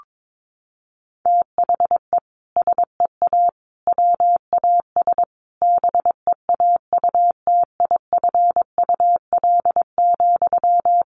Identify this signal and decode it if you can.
Morse code, heard as a keyed tone, about 22 wpm, 700 Hz. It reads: THE SEA WAS BEAUTIFUL,
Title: Sentences from Top 1200 Words 22wpm